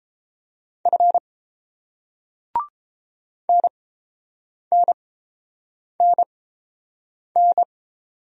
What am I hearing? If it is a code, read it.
Morse code, heard as a keyed tone, about 30 wpm, spaced as for 7 wpm, 690 Hz. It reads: F NNNN